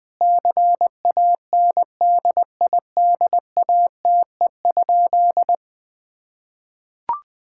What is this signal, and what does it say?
Morse code, heard as a keyed tone, about 20 wpm, 700 Hz. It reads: CANDIDATE?
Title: Sentences from Top 1600 Words 20wpm